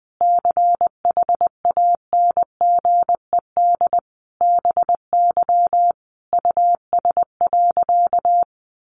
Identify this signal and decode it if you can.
Morse code, heard as a keyed tone, about 20 wpm, 700 Hz. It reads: CHANGED BY US.
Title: A Sound Of Thunder 20wpm